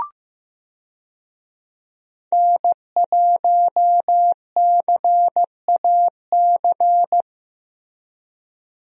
Morse code, heard as a keyed tone, about 15 wpm, 700 Hz. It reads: N1CAC
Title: USA Call Signs - 5 Characters - Repeated 2x - 15wpm